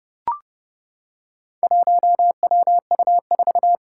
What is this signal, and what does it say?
Morse code, heard as a keyed tone, about 30 wpm, 700 Hz. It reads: 1WU4
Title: Four Challenging Letter-Number Combinations 30wpm